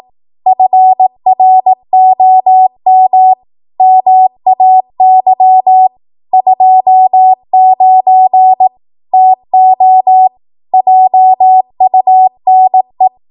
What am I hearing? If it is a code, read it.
Morse code, heard as a keyed tone, about 18 wpm, 750 Hz. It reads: FROM MAY 29 TO JUNE
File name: Morse / 18WPM